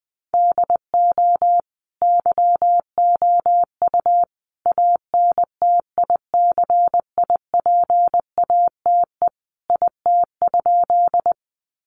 Morse code, heard as a keyed tone, about 20 wpm, 700 Hz. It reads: DO YOU ANTICIPATE IT?
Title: Sentences from Top 1500 Words 20wpm